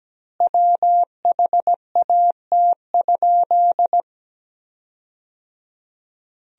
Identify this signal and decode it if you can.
Morse code, heard as a keyed tone, about 17 wpm, 700 Hz. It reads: WHAT?
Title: 3x Word-Spacing - Sentences from Top 300 Words - 17wpm